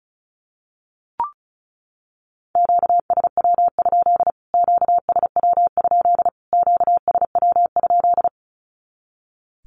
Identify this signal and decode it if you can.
Morse code, heard as a keyed tone, about 34 wpm, 700 Hz. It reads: QSW? QSW? QSW?
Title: Q-Codes Repeated 3x 35wpm